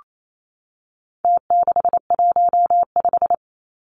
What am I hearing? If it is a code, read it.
Morse code, heard as a keyed tone, about 28 wpm, 700 Hz. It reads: T615
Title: Numbers - Contest Style - 4-Digit 28wpm